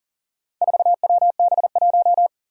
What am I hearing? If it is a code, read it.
Morse code, heard as a keyed tone, about 37 wpm, 700 Hz. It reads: 4WB1
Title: Four Challenging Letter-Number Combinations 40wpm